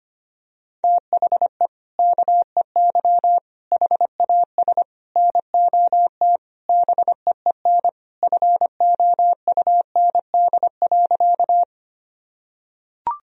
Morse code, heard as a keyed tone, about 25 wpm, 700 Hz. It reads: THE KEY HAS NOT BEEN FOUND.
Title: Sentences from Top 1000 Words - Review Part 4 - 25wpm